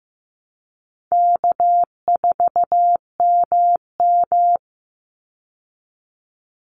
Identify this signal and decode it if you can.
Morse code, heard as a keyed tone, about 15 wpm, 700 Hz. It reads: K4MM
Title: Easy Call Signs Repeated 2x - 15wpm